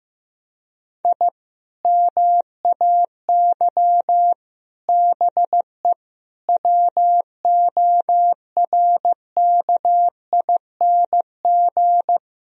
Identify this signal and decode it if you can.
Morse code, heard as a keyed tone, about 15 wpm, 700 Hz. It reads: I MAY BE WORKING
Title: Sentences from Top 1000 Words - Review Part 4 - 15wpm